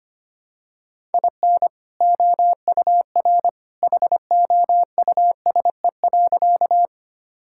Morse code, heard as a keyed tone, about 25 wpm, 700 Hz. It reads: IN OUR HOUSE.
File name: practice / 25wpm / Sentences with 2-Word Phrases - 25wpm